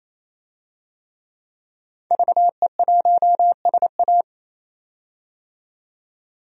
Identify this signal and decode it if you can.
Morse code, heard as a keyed tone, about 28 wpm, 700 Hz. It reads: VE1SA
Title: DXCC-100 Call Signs - 5 Characters - 28wpm